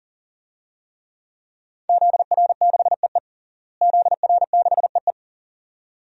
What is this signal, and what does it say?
Morse code, heard as a keyed tone, about 40 wpm, 700 Hz. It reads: ZR6EE ZR6EE
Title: DXCC-100 Call Signs - 5 Characters - Repeated 2x - 40wpm